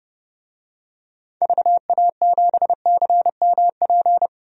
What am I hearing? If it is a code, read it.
Morse code, heard as a keyed tone, about 32 wpm, 700 Hz. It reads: VA7CMP
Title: DXCC-100 Call Signs - 6 Characters - 30wpm